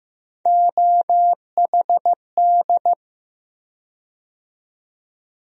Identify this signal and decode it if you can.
Morse code, heard as a keyed tone, about 15 wpm, 700 Hz. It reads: OHD
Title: USA Call Sign Postfixes - 3 Letters 15wpm